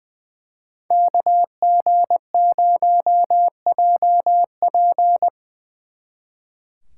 Morse code, heard as a keyed tone, about 22 wpm, 700 Hz. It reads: KG0JP
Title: Speed Racing - USA 2x2 Call Signs 20wpm